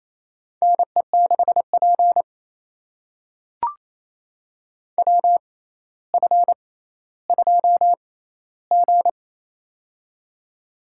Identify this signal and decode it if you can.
Morse code, heard as a keyed tone, about 27 wpm, spaced as for 14 wpm, 700 Hz. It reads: NE6P WF2G
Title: USA Call Signs - 4 Characters - 6x Character Spacing - 28wpm